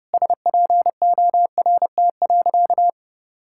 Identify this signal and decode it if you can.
Morse code, heard as a keyed tone, about 30 wpm, 700 Hz. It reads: SPORT.
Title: Sentences from Top 1600 Words 30wpm